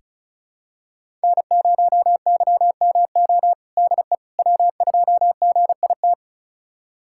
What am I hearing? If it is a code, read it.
Morse code, heard as a keyed tone, about 34 wpm, 700 Hz. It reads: N0YMO DE W2GIT
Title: USA Callsign DE Callsign - 5 Characters - 35wpm